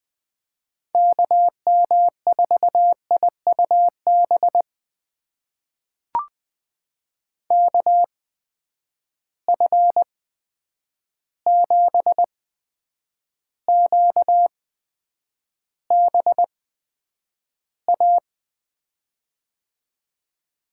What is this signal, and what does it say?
Morse code, heard as a keyed tone, about 19 wpm, spaced as for 9 wpm, 700 Hz. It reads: KM4IUB KF7QBA
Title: USA Call Signs - 6 Characters - 8x Character Spacing - 20wpm